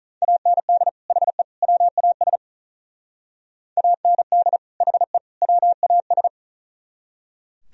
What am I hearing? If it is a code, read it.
Morse code, heard as a keyed tone, about 45 wpm, 690 Hz. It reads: AND HE WAS AND HE WAS